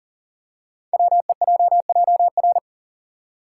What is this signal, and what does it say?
Morse code, heard as a keyed tone, about 39 wpm, 700 Hz. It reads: WEJJR
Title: Code Groups - 5 Letters - 40wpm